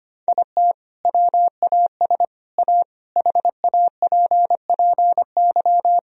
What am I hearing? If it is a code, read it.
Morse code, heard as a keyed tone, about 25 wpm, 700 Hz. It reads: IT WAS A HAPPY
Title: Sentences from Top 1000 Words - Review Part 1 - 25wpm